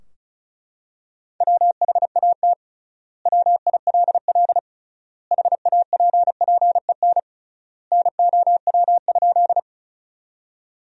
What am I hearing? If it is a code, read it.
Morse code, heard as a keyed tone, about 34 wpm, 700 Hz. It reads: WHAT WILL HAPPEN NOW?
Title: 3x Word-Spacing - Sentences from Top 400 Words - 35wpm